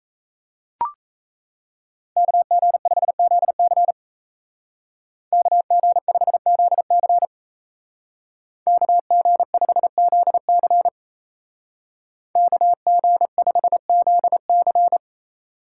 Morse code, heard as a keyed tone, about 33 wpm, 700 Hz. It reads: KG5ZC KG5ZC KG5ZC KG5ZC
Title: Speed Racing - USA 2x2 Call Signs 28wpm